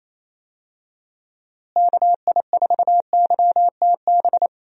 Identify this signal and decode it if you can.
Morse code, heard as a keyed tone, about 28 wpm, 700 Hz. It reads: KI4YTB
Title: USA Call Signs - 6 Characters - Repeated 2x - 28wpm